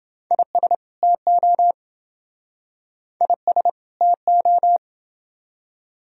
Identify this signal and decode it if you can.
Morse code, heard as a keyed tone, about 28 wpm, 700 Hz. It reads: IS TO IS TO